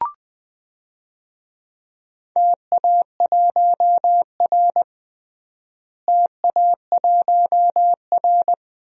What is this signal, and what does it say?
Morse code, heard as a keyed tone, about 21 wpm, 700 Hz. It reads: TA1R TA1R
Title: DXCC-100 Call Signs - 4 Characters - Repeated 2x - 20wpm